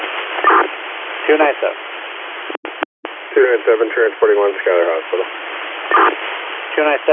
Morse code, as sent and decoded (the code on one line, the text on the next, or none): none